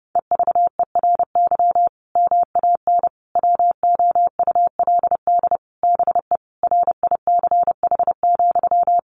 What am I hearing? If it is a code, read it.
Morse code, heard as a keyed tone, about 30 wpm, 700 Hz. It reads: EVERY MAN WOULD BE RICH,
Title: Sentences from Top 1500 Words 30wpm